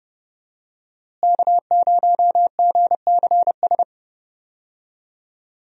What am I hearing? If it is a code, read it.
Morse code, heard as a keyed tone, about 30 wpm, 700 Hz. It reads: K0GCS
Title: USA Call Signs - 5 Characters - Repeated 2x - 30wpm